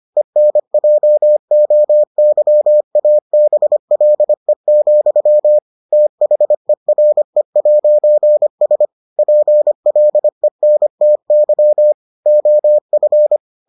Morse code, heard as a keyed tone, about 25 wpm, 600 Hz. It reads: ENJOYABLE, THERE'S PLENTY OF